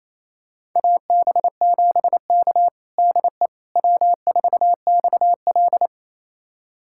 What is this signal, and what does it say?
Morse code, heard as a keyed tone, about 28 wpm, 700 Hz. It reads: AB7K DE W4XL
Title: USA Callsign DE Callsign - 4 Characters - 28wpm